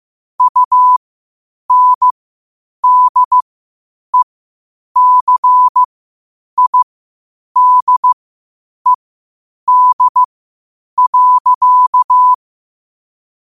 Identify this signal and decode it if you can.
Morse code, heard as a keyed tone, about 15 wpm, spaced as for 8 wpm, 1000 Hz. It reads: UNDECIDED.
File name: qotdfiles / 8wpm